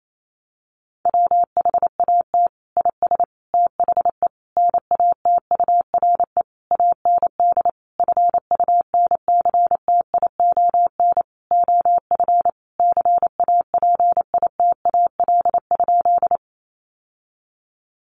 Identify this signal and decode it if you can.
Morse code, heard as a keyed tone, about 28 wpm, 700 Hz. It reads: WHAT IS THE NATURE AND FUNCTION OF CAPITAL?
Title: Sentences from Top 1100 Words 28wpm